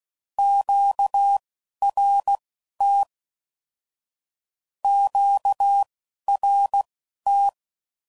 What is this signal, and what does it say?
Morse code, heard as a keyed tone, about 16 wpm, spaced as for 11 wpm, 800 Hz. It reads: QRT QRT
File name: CW Code Groups @10WPM / Q Calls / Q Calls Twice